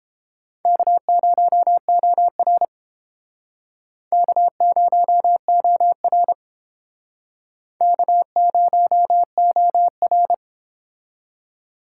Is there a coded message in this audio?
Yes: Morse code from 0.7 to 10.3 s, heard as a keyed tone, about 32 wpm, 700 Hz. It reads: K0OR K0OR K0OR